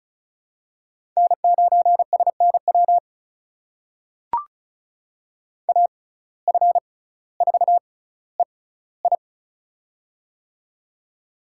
Morse code, heard as a keyed tone, about 34 wpm, spaced as for 18 wpm, 700 Hz. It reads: N9SNW AF4EI
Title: USA Call Signs - 5 Characters - 6x Character Spacing - 35wpm